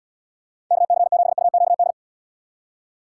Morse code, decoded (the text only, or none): KC7UQD